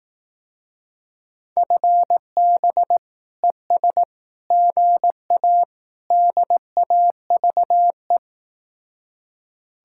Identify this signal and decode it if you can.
Morse code, heard as a keyed tone, about 18 wpm, 700 Hz. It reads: FB ES GA DAVE
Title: Speed-Racing Common QSO Phrases Encore - 15wpm